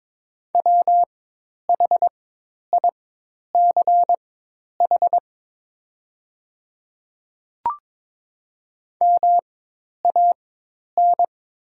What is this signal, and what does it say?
Morse code, heard as a keyed tone, about 22 wpm, spaced as for 10 wpm, 700 Hz. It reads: WHICH MAN
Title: 4x Character Spacing - Top 500 Words - 22wpm